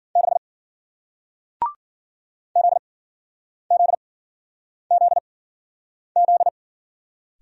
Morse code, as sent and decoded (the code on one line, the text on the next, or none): -.... / --.. --.. --.. --..
6 ZZZZ